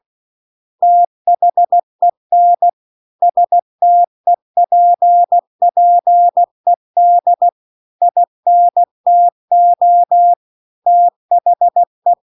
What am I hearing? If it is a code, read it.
Morse code, heard as a keyed tone, about 16 wpm, 700 Hz. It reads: THEN STEPPED INTO THE
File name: CW-The-Big-Twoheaded-River-2-16WPM